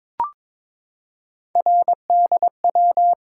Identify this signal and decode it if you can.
Morse code, heard as a keyed tone, about 22 wpm, 700 Hz. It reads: RDW
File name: practice / 22wpm / Three Letter-Number Combinations - 22wpm